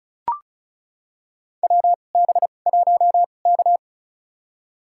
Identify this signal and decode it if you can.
Morse code, heard as a keyed tone, about 36 wpm, spaced as for 25 wpm, 700 Hz. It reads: WB1K
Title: USA Call Signs - 2x Character Spacing - 35wpm